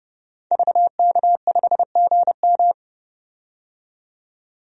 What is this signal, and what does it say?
Morse code, heard as a keyed tone, about 30 wpm, 700 Hz. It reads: VK5GM
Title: DXCC-100 Call Signs - 5 Characters - 30wpm